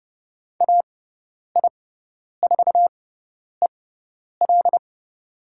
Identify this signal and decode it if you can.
Morse code, heard as a keyed tone, about 30 wpm, spaced as for 10 wpm, 700 Hz. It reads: AI4EL
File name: Lesson 18g - Letter m - Call Signs 30wpm